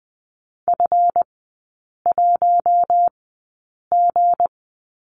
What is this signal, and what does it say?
Morse code, heard as a keyed tone, about 20 wpm, spaced as for 8 wpm, 690 Hz. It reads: F1G
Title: 159 Lesson 26e - Letter g - Three Characters 20wpm